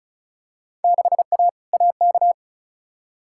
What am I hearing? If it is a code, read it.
Morse code, heard as a keyed tone, about 36 wpm, 700 Hz. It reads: 6A AK
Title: Field Day Exchanges Encore - 35wpm